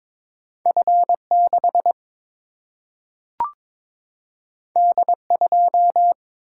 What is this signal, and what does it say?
Morse code, heard as a keyed tone, about 22 wpm, 700 Hz. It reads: F6 D2
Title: Single-Letter Single-Number 22wpm